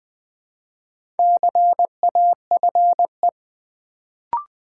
Morse code, heard as a keyed tone, about 20 wpm, 690 Hz. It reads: CAFE
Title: Build A Word - 2 Letters At A Time - 4-Letter Words - 20wpm